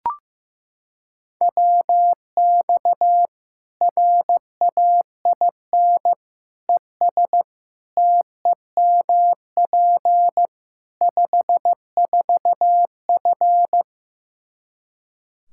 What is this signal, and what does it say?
Morse code, heard as a keyed tone, about 16 wpm, 700 Hz. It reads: WX RAIN ES TEMP 54F
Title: Practice QSO - Exchange 2 - Info - 15wpm